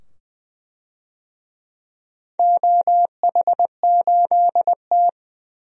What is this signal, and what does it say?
Morse code, heard as a keyed tone, about 20 wpm, 700 Hz. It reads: OH8T
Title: DXCC-100 Call Signs - 4 Characters - 20wpm